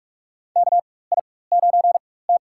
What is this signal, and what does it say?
Morse code, heard as a keyed tone, about 46 wpm, spaced as for 21 wpm, 700 Hz. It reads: KI9T
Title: USA Call Signs - 4 Characters - 4x Character Spacing - 45wpm